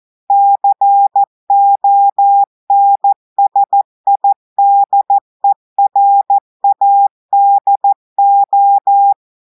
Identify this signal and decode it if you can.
Morse code, heard as a keyed tone, about 14 wpm, 800 Hz. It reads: CONSIDERADO
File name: audios / portugais / portugais texte en clair 1 vitesse 14 mots minutes